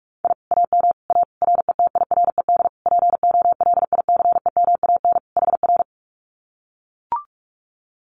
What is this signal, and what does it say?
Morse code, heard as a keyed tone, about 44 wpm, 700 Hz. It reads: I AM A RETIRED POLICEMAN HR
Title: Common QSO Phrases Encore 45wpm